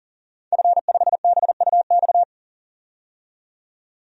Morse code, heard as a keyed tone, about 40 wpm, 700 Hz. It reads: F5BUX
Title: DXCC-100 Call Signs - 5 Characters - 40wpm